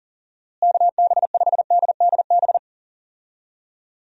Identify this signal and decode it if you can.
Morse code, heard as a keyed tone, about 40 wpm, 700 Hz. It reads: KB5DDB
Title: USA Call Signs - 6 Characters - Repeated 2x - 40wpm